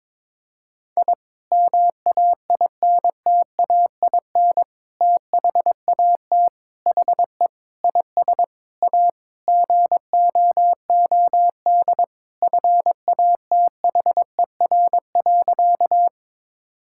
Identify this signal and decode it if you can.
Morse code, heard as a keyed tone, about 22 wpm, 700 Hz. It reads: I MAINTAIN THAT HE IS A GOOD FATHER.
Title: Sentences from Top 1000 Words - Review Part 5 - 22wpm